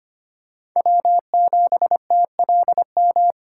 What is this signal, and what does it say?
Morse code, heard as a keyed tone, about 25 wpm, 700 Hz. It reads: W7TLM
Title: USA Call Signs - 5 Characters - 3x Character Spacing - 25wpm